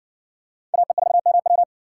Morse code, heard as a keyed tone, about 53 wpm, 700 Hz. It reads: AE4MW